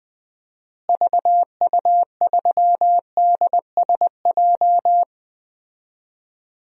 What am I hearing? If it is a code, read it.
Morse code, heard as a keyed tone, about 20 wpm, 700 Hz. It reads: VU3DSJ
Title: DXCC-100 Call Signs - 6 Characters - Repeated 3x - 20wpm